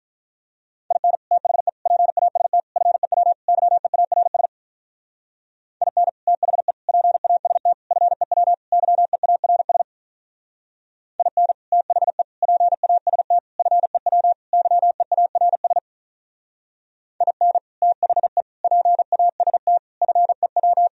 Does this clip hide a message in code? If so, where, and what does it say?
Yes: Morse code from 0.9 to 20.9 s, heard as a keyed tone, about 45 wpm, 700 Hz. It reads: IN THE PAST FEW YEARS IN THE PAST FEW YEARS IN THE PAST FEW YEARS IN THE PAST FEW